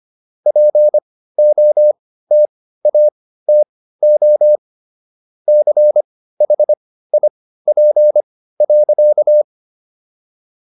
Morse code, heard as a keyed tone, about 25 wpm, spaced as for 15 wpm, 600 Hz. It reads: POTATO CHIP.